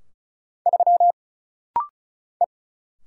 Morse code, heard as a keyed tone, about 36 wpm, 700 Hz. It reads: 3 E
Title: Single Letter-Number Alt - Rapid-Fire 35wpm